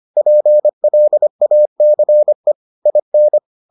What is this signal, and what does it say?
Morse code, heard as a keyed tone, about 25 wpm, 600 Hz. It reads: PLACE IN